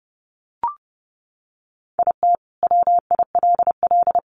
Morse code, heard as a keyed tone, about 30 wpm, 700 Hz. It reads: IT WILL
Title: Sets of 2 Words 30wpm